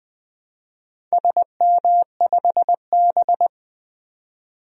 Morse code, heard as a keyed tone, about 20 wpm, 700 Hz. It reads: SM5B
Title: DXCC-100 Call Signs - 4 Characters - 20wpm